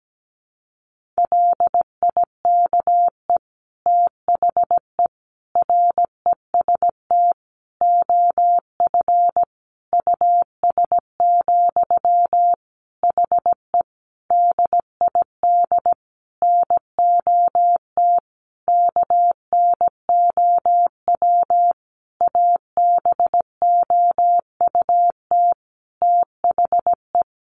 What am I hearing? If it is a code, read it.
Morse code, heard as a keyed tone, about 17 wpm, 700 Hz. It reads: LIKE THE REST OF US, HE DID NOT KNOW ABOUT THE
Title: Sentences from Top 1400 Words 17wpm